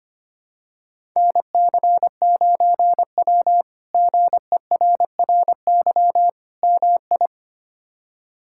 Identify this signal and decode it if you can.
Morse code, heard as a keyed tone, about 25 wpm, 700 Hz. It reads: NC9W GERRY MI
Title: K1USN SST QSO Practice - 25wpm